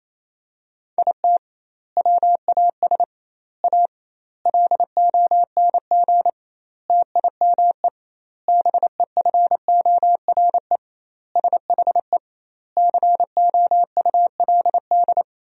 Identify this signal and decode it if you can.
Morse code, heard as a keyed tone, about 28 wpm, 700 Hz. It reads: IT WAS A LONG TIME BEFORE SHE COULD
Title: 2x Word-Spacing - Sentences from Top 500 Words - 28wpm